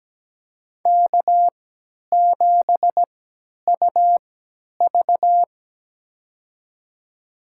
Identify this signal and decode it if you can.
Morse code, heard as a keyed tone, about 17 wpm, spaced as for 10 wpm, 700 Hz. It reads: K7UV
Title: USA Call Signs - 4 Characters - 3x Character Spacing - 17wpm